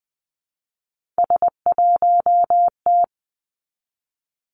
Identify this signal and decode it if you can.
Morse code, heard as a keyed tone, about 20 wpm, 700 Hz. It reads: S1T
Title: Three Challenging Letter-Number Combinations 20wpm